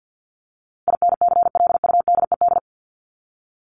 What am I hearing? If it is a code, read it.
Morse code, heard as a keyed tone, about 50 wpm, 700 Hz. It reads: INCLUDED
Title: ED Suffix - Top 100 Words with Base Words 50wpm